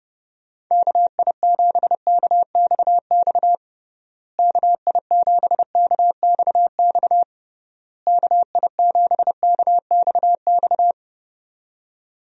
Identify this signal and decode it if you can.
Morse code, heard as a keyed tone, about 32 wpm, 700 Hz. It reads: KI7KXX KI7KXX KI7KXX
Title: USA Call Signs - 6 Characters - Repeated 3x - 30wpm